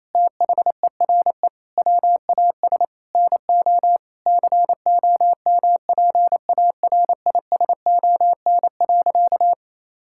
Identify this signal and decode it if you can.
Morse code, heard as a keyed tone, about 28 wpm, 700 Hz. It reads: THERE WAS NO COMPARISON.
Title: Sentences from Top 1600 Words 28wpm